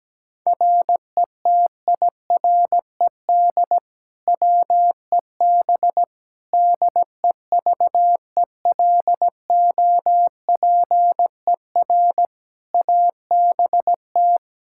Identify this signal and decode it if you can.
Morse code, heard as a keyed tone, about 17 wpm, 700 Hz. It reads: RETIRED WEB DEVELOPER ABT
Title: Practice QSO - Exchange 3 - More Info - 17wpm